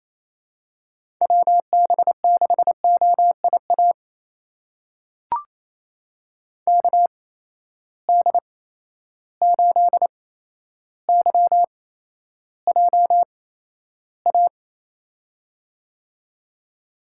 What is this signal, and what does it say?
Morse code, heard as a keyed tone, about 27 wpm, spaced as for 12 wpm, 700 Hz. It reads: WB6OIA KD8YJA
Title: USA Call Signs - 8x Character Spacing - 28wpm